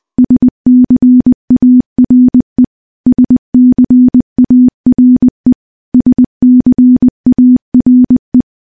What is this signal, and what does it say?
Morse code, heard as a keyed tone, about 21 wpm, 250 Hz. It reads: SCARE SCARE SCARE
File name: morse-training-20WPM